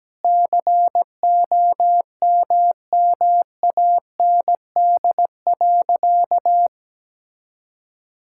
Morse code, heard as a keyed tone, about 17 wpm, 700 Hz. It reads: COMMAND.
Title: Sentences from Top 1600 Words 17wpm